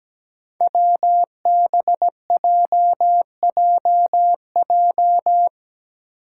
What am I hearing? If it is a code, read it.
Morse code, heard as a keyed tone, about 17 wpm, 700 Hz. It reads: WBJJJ
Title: Five Challenging Letter-Number Combinations 17wpm